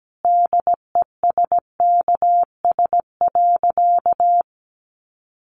Morse code, heard as a keyed tone, about 17 wpm, 700 Hz. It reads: DESKS.
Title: Additional Easy Phrases 17wpm